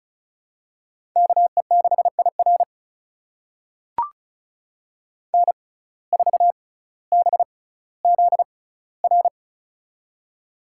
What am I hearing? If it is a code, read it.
Morse code, heard as a keyed tone, about 36 wpm, spaced as for 18 wpm, 700 Hz. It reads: KE6IR N4BZR
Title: USA Call Signs - 5 Characters - 6x Character Spacing - 35wpm